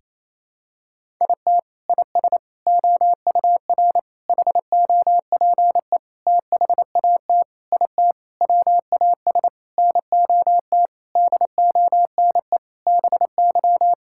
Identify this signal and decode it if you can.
Morse code, heard as a keyed tone, about 28 wpm, 700 Hz. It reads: IT IS OUR HOPE THAT IT WAS NOT DONE BY
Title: Sentences from Top 1000 Words - Review Part 5 - 28wpm